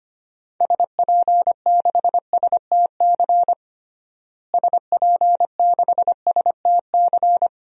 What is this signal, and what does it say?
Morse code, heard as a keyed tone, about 25 wpm, 700 Hz. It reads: SP6STC SP6STC
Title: DXCC-100 Call Signs - 6 Characters - Repeated 3x - 25wpm